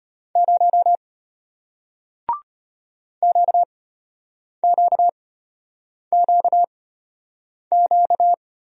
Morse code, heard as a keyed tone, about 34 wpm, spaced as for 8 wpm, 700 Hz. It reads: 0 QQQQ